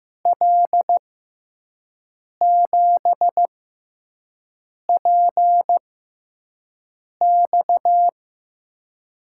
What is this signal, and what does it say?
Morse code, heard as a keyed tone, about 15 wpm, spaced as for 5 wpm, 700 Hz. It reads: L7PX
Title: USA Call Signs - 5 Characters - 6x Character Spacing - 15wpm